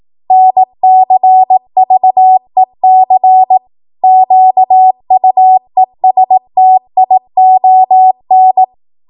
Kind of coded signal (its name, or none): Morse code